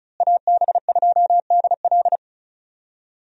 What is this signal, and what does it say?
Morse code, heard as a keyed tone, about 36 wpm, 700 Hz. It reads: AB2DL
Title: USA Call Signs - 5 Characters - Repeated 2x - 35wpm